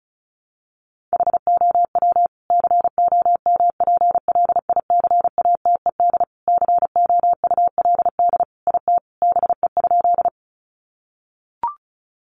Morse code, heard as a keyed tone, about 34 wpm, 700 Hz. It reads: HOW COMPLICATED COULD IT BE?
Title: Sentences from Top 1200 Words 35wpm